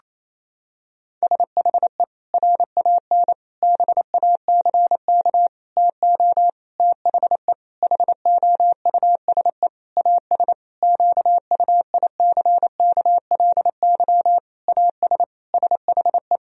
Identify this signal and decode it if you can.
Morse code, heard as a keyed tone, about 27 wpm, 700 Hz. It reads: SHE RAN BACK TO THE HOUSE AS QUICKLY AS SHE
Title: Sentences from Top 1000 Words - Review Part 3 - 28wpm